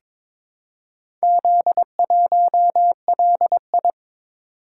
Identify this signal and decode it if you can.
Morse code, heard as a keyed tone, about 22 wpm, 700 Hz. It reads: Z1LI
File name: practice / 22wpm / Four Letter-Number Combinations - 22wpm